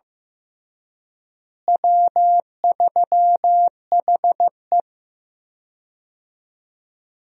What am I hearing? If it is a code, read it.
Morse code, heard as a keyed tone, about 15 wpm, 700 Hz. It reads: W3HE
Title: USA Call Signs - 4 Characters - Repeated 3x - 15wpm